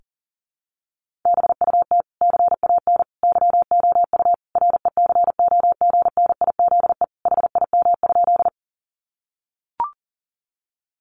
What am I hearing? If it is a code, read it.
Morse code, heard as a keyed tone, about 40 wpm, 700 Hz. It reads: BUT CAN YOU RECOGNIZE HIM?
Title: Sentences from Top 1400 Words 40wpm